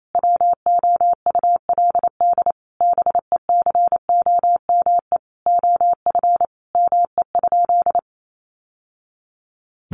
Morse code, heard as a keyed tone, about 28 wpm, 700 Hz. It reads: WOULD BECOME OF ME?
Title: Sentences from Top 600 Words 28wpm